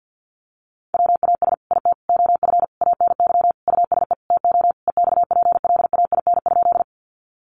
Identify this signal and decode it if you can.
Morse code, heard as a keyed tone, about 50 wpm, 700 Hz. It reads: WAS IT OF ANY USE TO EXPLAIN?
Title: Sentences from Top 1100 Words 50wpm